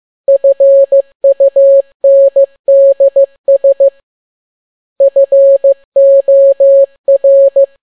Morse code, heard as a keyed tone, about 15 wpm, 550 Hz. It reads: FUNDS FOR
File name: n6tt p1 15C13